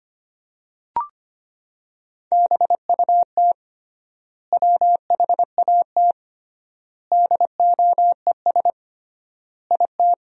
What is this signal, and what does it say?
Morse code, heard as a keyed tone, about 25 wpm, 700 Hz. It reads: BUT WHAT DOES IT
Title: 3x Word-Spacing - Sentences from Top 200 Words - 25wpm